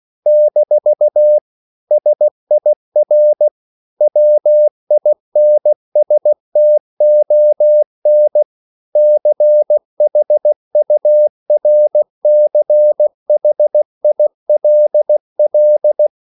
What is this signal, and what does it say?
Morse code, heard as a keyed tone, about 16 wpm, 600 Hz. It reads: - SIR WINSTON CHURCHILL
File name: CW / Practice / EnglishQuotations0416